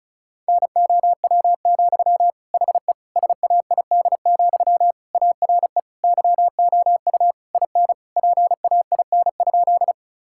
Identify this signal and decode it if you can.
Morse code, heard as a keyed tone, about 36 wpm, 700 Hz. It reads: NOW, HE SAID, ARE YOU IN PAIN?
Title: Sentences from Top 1600 Words 35wpm